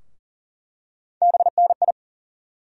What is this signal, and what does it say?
Morse code, heard as a keyed tone, about 37 wpm, 700 Hz. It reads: BNI
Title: USA Call Sign Postfixes - 3 Letters 40wpm